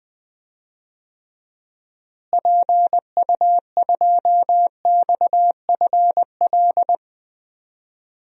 Morse code, heard as a keyed tone, about 20 wpm, 700 Hz. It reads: PU2XFL
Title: DXCC-100 Call Signs - 6 Characters - Repeated 2x - 20wpm